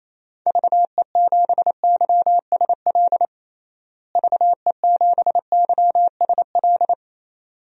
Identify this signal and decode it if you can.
Morse code, heard as a keyed tone, about 27 wpm, 700 Hz. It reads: VE7YSL VE7YSL
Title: DXCC-100 Call Signs - 6 Characters - Repeated 2x - 28wpm